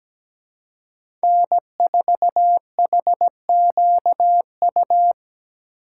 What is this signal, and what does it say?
Morse code, heard as a keyed tone, about 17 wpm, 700 Hz. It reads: N4HQU
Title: USA Call Signs - 5 Characters - 2x Character Spacing - 17wpm